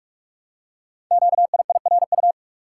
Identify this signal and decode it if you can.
Morse code, heard as a keyed tone, about 44 wpm, 690 Hz. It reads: QIIRU
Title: Code Groups - 5 Letters - 45wpm